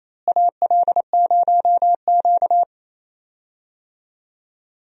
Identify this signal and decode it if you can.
Morse code, heard as a keyed tone, about 27 wpm, 700 Hz. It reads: AL0Q
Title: USA Call Signs - 4 Characters - Repeated 3x - 28wpm